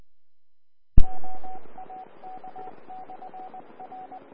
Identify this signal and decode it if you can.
Morse code, heard as a keyed tone, about 20 wpm, 750 Hz. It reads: OAK/R
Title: Austin, TX Amateur Repeaters